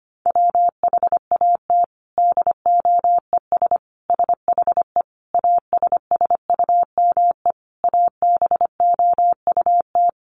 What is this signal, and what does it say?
Morse code, heard as a keyed tone, about 25 wpm, 700 Hz. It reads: WHAT DOES SHE ASSUME ABOUT
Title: Sentences from Top 1500 Words 25wpm